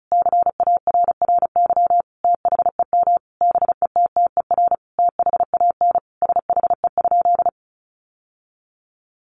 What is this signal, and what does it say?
Morse code, heard as a keyed tone, about 34 wpm, 700 Hz. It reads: CARRY THEM BETTER THAN SHE?
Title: Sentences from Top 1200 Words 35wpm